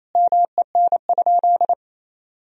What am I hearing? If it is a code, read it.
Morse code, heard as a keyed tone, about 28 wpm, 700 Hz. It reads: MEN?
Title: Sentences from Top 1000 Words - Review Part 2 - 28wpm